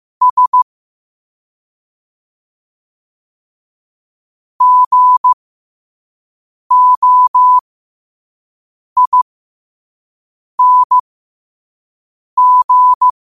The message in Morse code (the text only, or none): S GOING